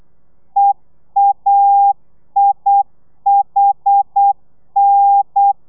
Morse code, heard as a keyed tone, about 8 wpm, 800 Hz. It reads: EAIHN